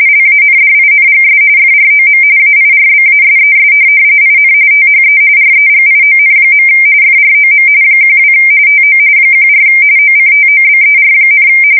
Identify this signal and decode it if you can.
RTTY, about 45.45 baud. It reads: LANDINGS FROM AIRBASES IN THE REGION. FROM 12 TO 14 OCTOBER,
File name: fsk170 / USS-Massachusetts